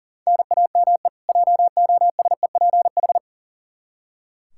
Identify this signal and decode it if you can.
Morse code, heard as a keyed tone, about 40 wpm, 700 Hz. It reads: NAME JOSEPH
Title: K1USN SST QSO Practice 40wpm